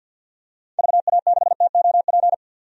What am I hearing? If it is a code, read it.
Morse code, heard as a keyed tone, about 50 wpm, 700 Hz. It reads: VA6TOP
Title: DXCC-100 Call Signs - 6 Characters - 50wpm